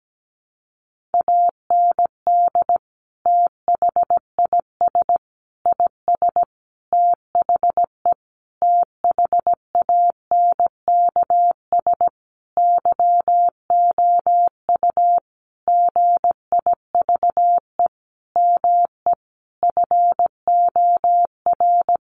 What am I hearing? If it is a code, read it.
Morse code, heard as a keyed tone, about 17 wpm, 700 Hz. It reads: AND THIS IS THE THANKS YOU GIVE ME FOR
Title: Sentences from Top 1200 Words 17wpm